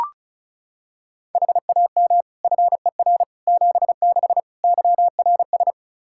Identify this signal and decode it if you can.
Morse code, heard as a keyed tone, about 35 wpm, 700 Hz. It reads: HAM FER 76 YRS
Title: K1USN SST QSO Practice 35wpm